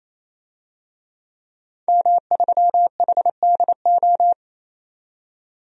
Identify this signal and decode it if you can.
Morse code, heard as a keyed tone, about 28 wpm, 700 Hz. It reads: M3HDO
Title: DXCC-100 Call Signs - 5 Characters - 28wpm